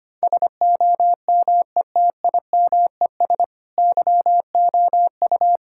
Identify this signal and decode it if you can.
Morse code, heard as a keyed tone, about 25 wpm, 700 Hz. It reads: SOMETIMES YOU
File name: practice / 25wpm / Sentences with 3-Word Phrases - 25wpm